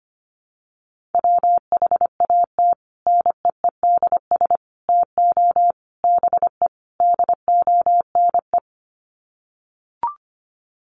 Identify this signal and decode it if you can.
Morse code, heard as a keyed tone, about 25 wpm, 700 Hz. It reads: WHAT NEEDS TO BE DONE
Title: Sets of 5 Words Encore 25wpm